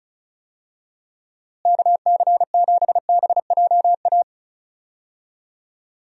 Morse code, heard as a keyed tone, about 36 wpm, 700 Hz. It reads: KC7BJA
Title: USA Call Signs - 6 Characters - Repeated 3x - 35wpm